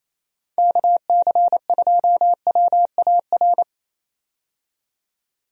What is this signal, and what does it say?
Morse code, heard as a keyed tone, about 28 wpm, 700 Hz. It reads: KC2WAR
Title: USA Call Signs - 6 Characters - 28wpm